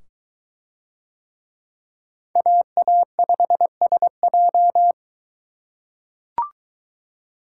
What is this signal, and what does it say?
Morse code, heard as a keyed tone, about 21 wpm, 700 Hz. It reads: AA5SJ